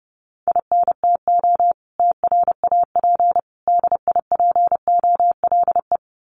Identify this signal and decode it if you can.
Morse code, heard as a keyed tone, about 30 wpm, 700 Hz. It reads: INTO TRAP DIPOLE
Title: Common QSO Phrases 30wpm